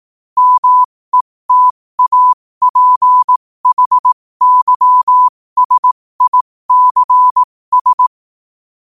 Morse code, heard as a keyed tone, about 18 wpm, spaced as for 15 wpm, 1000 Hz. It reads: METAPHYSICS
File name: qotdfiles / 15wpm